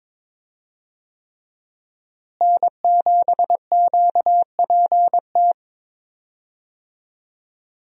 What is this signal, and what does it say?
Morse code, heard as a keyed tone, about 21 wpm, 700 Hz. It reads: N7QPT
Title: USA Call Signs - 5 Characters - Repeated 3x - 22wpm